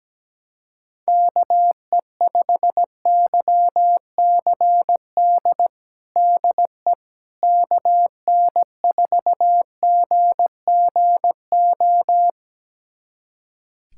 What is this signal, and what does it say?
Morse code, heard as a keyed tone, about 17 wpm, 700 Hz. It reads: KE5YCD DE KN4GGO
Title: USA Callsign DE Callsign - 6 Characters - 17wpm